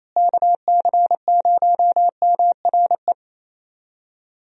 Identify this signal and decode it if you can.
Morse code, heard as a keyed tone, about 28 wpm, 700 Hz. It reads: KC0MRE
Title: USA Call Signs - 6 Characters - Repeated 2x - 28wpm